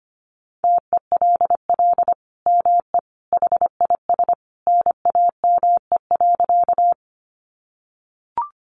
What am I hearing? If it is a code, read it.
Morse code, heard as a keyed tone, about 25 wpm, 700 Hz. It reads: TELL ME HIS NAME.
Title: Sentences from Top 200 Words 25wpm